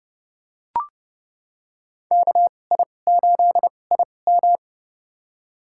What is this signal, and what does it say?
Morse code, heard as a keyed tone, about 32 wpm, spaced as for 22 wpm, 700 Hz. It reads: KI8IM
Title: USA Call Signs - 5 Characters - 2x Character Spacing - 30wpm